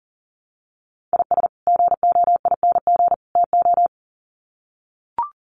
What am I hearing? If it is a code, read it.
Morse code, heard as a keyed tone, about 40 wpm, 700 Hz. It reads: IS GOING TO
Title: Sets of 3 Words 40wpm